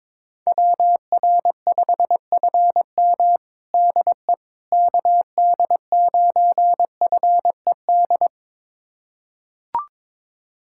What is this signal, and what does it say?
Morse code, heard as a keyed tone, about 22 wpm, 700 Hz. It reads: WR5FM DE KD9FED
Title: USA Callsign DE Callsign 22wpm